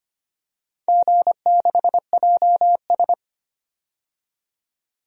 Morse code, heard as a keyed tone, about 25 wpm, 700 Hz. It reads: G6JS
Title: DXCC-100 Call Signs - 4 Characters - 25wpm